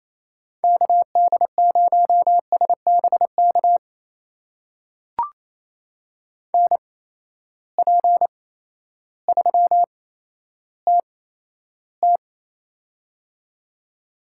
Morse code, heard as a keyed tone, about 28 wpm, spaced as for 13 wpm, 700 Hz. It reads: KD0SBK NP3TT
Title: USA Call Signs - 8x Character Spacing - 28wpm